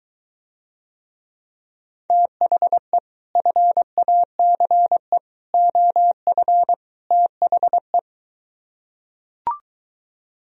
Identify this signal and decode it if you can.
Morse code, heard as a keyed tone, about 23 wpm, 700 Hz. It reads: THE FACE OF THE